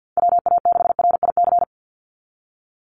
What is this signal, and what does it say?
Morse code, heard as a keyed tone, about 49 wpm, 700 Hz. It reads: WA6RIC
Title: NA SOTA Chaser Callsigns 50wpm